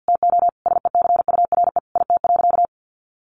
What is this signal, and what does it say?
Morse code, heard as a keyed tone, about 49 wpm, 700 Hz. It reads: TO SECURE IT.
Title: Sentences from Top 1200 Words 50wpm